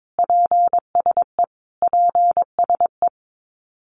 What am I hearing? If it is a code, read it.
Morse code, heard as a keyed tone, about 22 wpm, 700 Hz. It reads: PSE PSE
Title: Abbreviations Repeated 3x - 22wpm